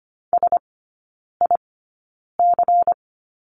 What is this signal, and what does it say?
Morse code, heard as a keyed tone, about 25 wpm, spaced as for 9 wpm, 700 Hz. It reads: SIC
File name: course / 25:10wpm / 093 Lesson 16e - Letter c - Three Characters 25wpm